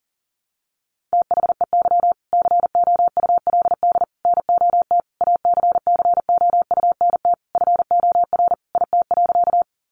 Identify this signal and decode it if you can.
Morse code, heard as a keyed tone, about 40 wpm, 700 Hz. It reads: THEY COULD NOT ACCOUNT FOR IT.
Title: Sentences from Top 900 Words 40wpm